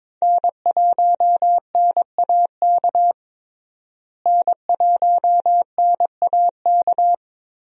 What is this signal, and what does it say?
Morse code, heard as a keyed tone, about 22 wpm, 700 Hz. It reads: N1NAK N1NAK
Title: USA Call Signs - 5 Characters - Repeated 2x - 22wpm